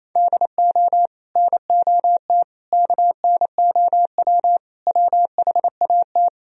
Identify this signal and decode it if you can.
Morse code, heard as a keyed tone, about 37 wpm, 700 Hz. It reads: DO NOT KNOW WHAT